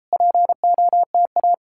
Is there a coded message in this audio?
Yes: Morse code from 0.1 to 1.5 s, heard as a keyed tone, about 39 wpm, 700 Hz. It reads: POTA